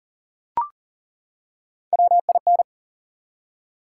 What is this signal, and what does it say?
Morse code, heard as a keyed tone, about 38 wpm, 700 Hz. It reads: WIN
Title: Three Letter Words - 40wpm